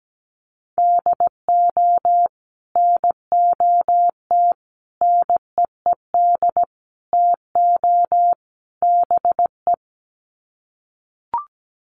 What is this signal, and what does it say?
Morse code, heard as a keyed tone, about 17 wpm, 700 Hz. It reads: DO NOT NEED TO BE
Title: Sets of 5 Words Encore 17wpm